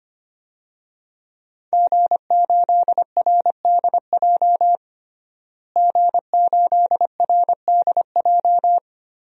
Morse code, heard as a keyed tone, about 25 wpm, 700 Hz. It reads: G8RDJ G8RDJ
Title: DXCC-100 Call Signs - 5 Characters - Repeated 2x - 25wpm